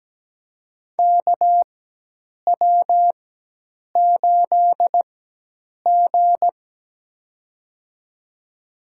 Morse code, heard as a keyed tone, about 17 wpm, spaced as for 8 wpm, 700 Hz. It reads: KW8G
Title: USA Call Signs - 4x Character Spacing - 17wpm